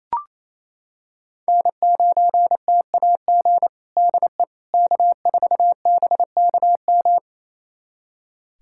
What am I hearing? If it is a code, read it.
Morse code, heard as a keyed tone, about 28 wpm, 700 Hz. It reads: N9TAG DE K4BKM
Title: USA Callsign DE Callsign - 5 Characters - 28wpm